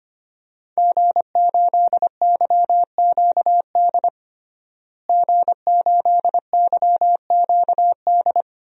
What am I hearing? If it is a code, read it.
Morse code, heard as a keyed tone, about 25 wpm, 700 Hz. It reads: G8YQD G8YQD
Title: DXCC-100 Call Signs - 5 Characters - Repeated 2x - 25wpm